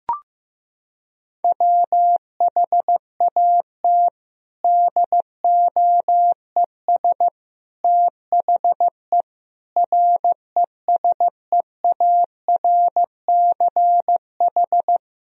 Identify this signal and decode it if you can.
Morse code, heard as a keyed tone, about 15 wpm, 700 Hz. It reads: WHAT DOES THE RESEARCH
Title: Sentences from Top 1000 Words - Review Part 4 - 15wpm